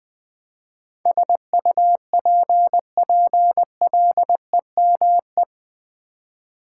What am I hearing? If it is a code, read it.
Morse code, heard as a keyed tone, about 21 wpm, 700 Hz. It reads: SUPPLEME
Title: Build A Word - 4 Letters At A Time - 10-Letter Words - 20wpm